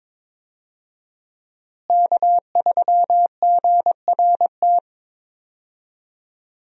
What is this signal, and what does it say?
Morse code, heard as a keyed tone, about 22 wpm, 700 Hz. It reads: K3GRT
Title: USA Call Signs - 5 Characters - Repeated 3x - 22wpm